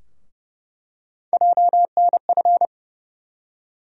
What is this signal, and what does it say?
Morse code, heard as a keyed tone, about 30 wpm, 700 Hz. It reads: JNF
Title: USA Call Sign Postfixes - 3 Letters 30wpm